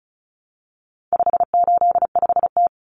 Morse code, heard as a keyed tone, about 34 wpm, 700 Hz. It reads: 585T
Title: Common Cut Numbers - 4-Digit 35wpm